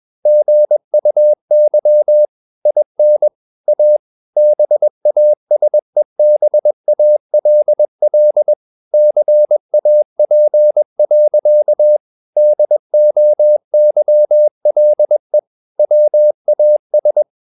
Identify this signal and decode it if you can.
Morse code, heard as a keyed tone, about 21 wpm, 600 Hz. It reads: GUY IN A BASEBALL CAP. DOYLE WAS
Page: CW med Gnister